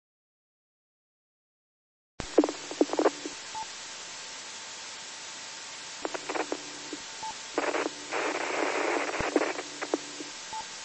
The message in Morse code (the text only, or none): none